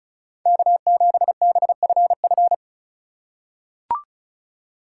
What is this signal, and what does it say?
Morse code, heard as a keyed tone, about 34 wpm, 700 Hz. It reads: K7BFF
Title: USA Call Signs - 5 Characters - 4x Character Spacing - 35wpm